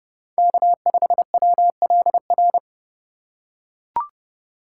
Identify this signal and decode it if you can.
Morse code, heard as a keyed tone, about 30 wpm, 700 Hz. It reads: K5WLR
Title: USA Call Signs - 5 Characters - 4x Character Spacing - 30wpm